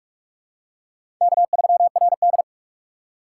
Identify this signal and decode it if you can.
Morse code, heard as a keyed tone, about 46 wpm, 700 Hz. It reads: K3RD
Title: USA Call Signs - 6x Character Spacing - 45wpm